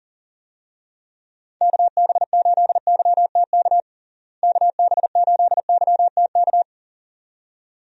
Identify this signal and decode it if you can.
Morse code, heard as a keyed tone, about 40 wpm, 700 Hz. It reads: KB8YTK KB8YTK
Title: USA Call Signs - 6 Characters - Repeated 2x - 40wpm